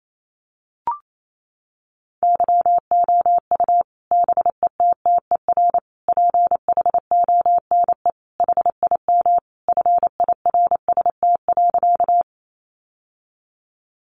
Morse code, heard as a keyed tone, about 28 wpm, 700 Hz. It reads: YOU BETTER PHONE HIM FIRST.
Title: Sentences from Top 1100 Words 28wpm